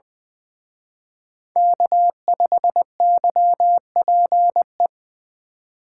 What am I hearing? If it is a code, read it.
Morse code, heard as a keyed tone, about 23 wpm, 700 Hz. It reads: K5YPE
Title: Speed Racing - USA 1x3 Call Signs 20wpm